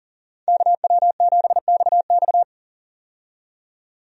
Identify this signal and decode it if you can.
Morse code, heard as a keyed tone, about 40 wpm, 700 Hz. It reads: KW7XX
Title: USA Call Signs - 5 Characters - 40wpm